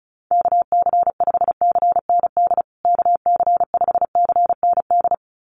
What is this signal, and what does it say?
Morse code, heard as a keyed tone, about 34 wpm, 700 Hz. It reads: KC5CND KC5CND
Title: USA Call Signs Repeated 2x 35wpm